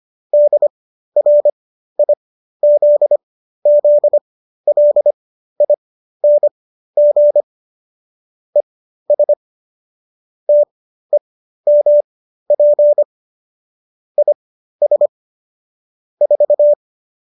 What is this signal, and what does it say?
Morse code, heard as a keyed tone, about 25 wpm, spaced as for 13 wpm, 590 Hz. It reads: DRIZZLING ES TEMP IS 4